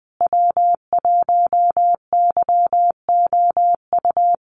Morse code, heard as a keyed tone, about 20 wpm, 700 Hz. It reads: W1YOU
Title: USA Call Signs Repeated 3x 20wpm